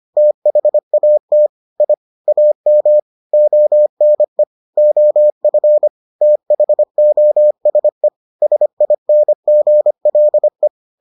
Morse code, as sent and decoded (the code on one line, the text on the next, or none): - .... .- - / .. / .- -- / --- -. / --- ..-. / - .... --- ... / ... .. -. --. .-.. .
THAT I AM ONE OF THOSE SINGLE